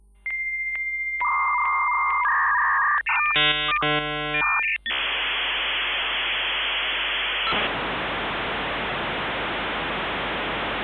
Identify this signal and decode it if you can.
Dial-up modem handshake.